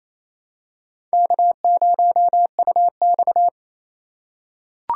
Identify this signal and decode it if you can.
Morse code, heard as a keyed tone, about 28 wpm, 700 Hz. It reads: K0UX
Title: USA Call Signs - 4 Characters - 2x Character Spacing - 28wpm